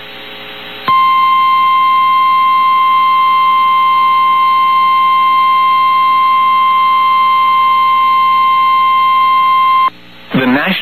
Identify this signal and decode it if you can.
Emergency Alert System (SAME) tones (the 1050 Hz attention tone), with speech. It is heard from 0.8 to 9.9 s.
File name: eas received alerts / L1 19552 audio